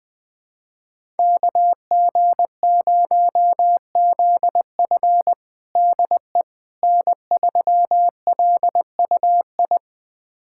Morse code, heard as a keyed tone, about 20 wpm, 700 Hz. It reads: KG0ZF DE N3LUI
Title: USA Callsign DE Callsign - 5 Characters - 20wpm